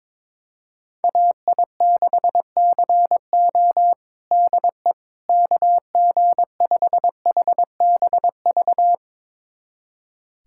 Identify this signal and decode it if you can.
Morse code, heard as a keyed tone, about 22 wpm, 700 Hz. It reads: AI6CO DE KG5HBV
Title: USA Callsign DE Callsign 22wpm